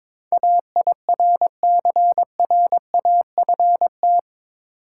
Morse code, heard as a keyed tone, about 22 wpm, 700 Hz. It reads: AIRCRAFT
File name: practice / 22wpm / Eight Letter Words - 22wpm